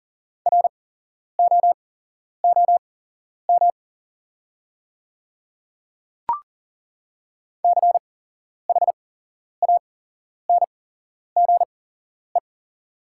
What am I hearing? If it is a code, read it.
Morse code, heard as a keyed tone, about 40 wpm, spaced as for 11 wpm, 690 Hz. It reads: ROOM CHANGE
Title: 8x Character Spacing - Top 300 Words - 40wpm